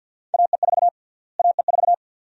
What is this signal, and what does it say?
Morse code, heard as a keyed tone, about 48 wpm, 700 Hz. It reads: AE4 AE4
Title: USA Call Sign Prefixes - Repeated 2x - 50wpm